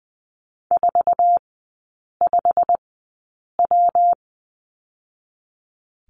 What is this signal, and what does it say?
Morse code, heard as a keyed tone, about 20 wpm, 700 Hz. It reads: 4 5 W
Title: 114 Lesson 19e - Letter w - Three Characters 20wpm